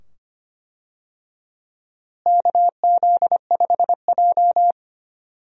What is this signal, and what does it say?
Morse code, heard as a keyed tone, about 25 wpm, 700 Hz. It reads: KZ5J
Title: USA Call Signs - 4 Characters - Repeated 2x - 25wpm